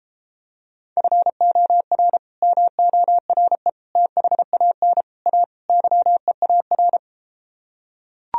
Morse code, heard as a keyed tone, about 33 wpm, 700 Hz. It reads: FOR MORE THAN A YEAR